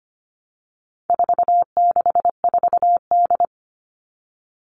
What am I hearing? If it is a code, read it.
Morse code, heard as a keyed tone, about 25 wpm, 700 Hz. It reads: 464D
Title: All Cut Numbers - 4-Digit 25wpm